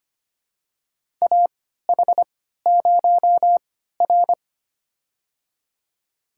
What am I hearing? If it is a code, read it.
Morse code, heard as a keyed tone, about 25 wpm, spaced as for 14 wpm, 700 Hz. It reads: AH0R
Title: USA Call Signs - 4 Characters - 3x Character Spacing - 25wpm